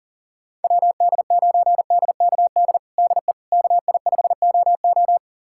Morse code, heard as a keyed tone, about 40 wpm, 700 Hz. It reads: WD9DKD DE KI5OO
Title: USA Callsign DE Callsign 40wpm